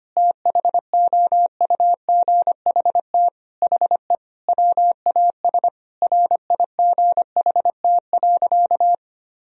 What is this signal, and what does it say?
Morse code, heard as a keyed tone, about 25 wpm, 700 Hz. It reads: THOUGHT HE WAS RIGHT.
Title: Sentences from Top 1000 Words - Review Part 4 - 25wpm